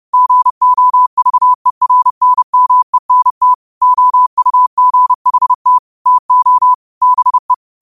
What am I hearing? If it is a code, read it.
Morse code, heard as a keyed tone, about 30 wpm, 1000 Hz. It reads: GOVERNMENT OUGHT TO BE
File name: qotdfiles / 30wpm